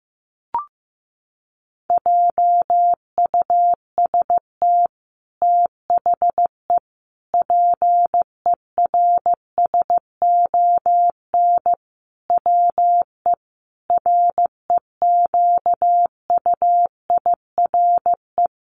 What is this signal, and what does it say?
Morse code, heard as a keyed tone, about 15 wpm, 700 Hz. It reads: JUST THE PERSON WE REQUIRE
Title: Sentences from Top 900 Words 15wpm